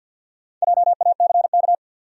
Morse code, heard as a keyed tone, about 53 wpm, 700 Hz. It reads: JACK
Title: USA Common Names - Max 4 Letters - 50wpm